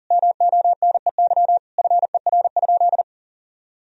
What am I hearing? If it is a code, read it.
Morse code, heard as a keyed tone, about 37 wpm, 700 Hz. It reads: MONEY FER?
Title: Sentences from Top 400 Words and Abbreviations 40wpm